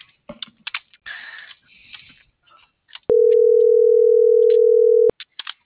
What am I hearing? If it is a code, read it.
Ringback tone (North America), 440 + 480 Hz.